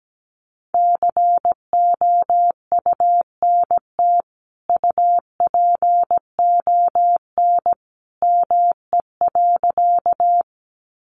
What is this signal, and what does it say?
Morse code, heard as a keyed tone, about 17 wpm, 700 Hz. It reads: COUNT UPON ME.
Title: Sentences from Top 1500 Words 17wpm